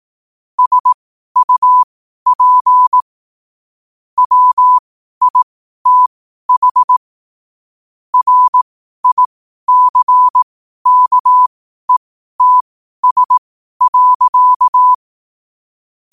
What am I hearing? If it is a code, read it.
Morse code, heard as a keyed tone, about 18 wpm, spaced as for 12 wpm, 1000 Hz. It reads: SUP WITH RICKETS.
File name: qotdfiles / 12wpm